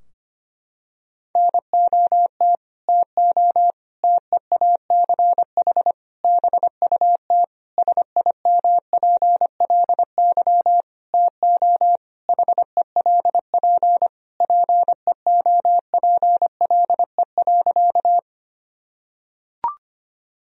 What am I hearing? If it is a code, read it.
Morse code, heard as a keyed tone, about 25 wpm, 700 Hz. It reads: NOT TO TEACH BUT SIMPLY TO HELP PEOPLE.
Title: Sentences from Top 1000 Words - Review Part 4 - 25wpm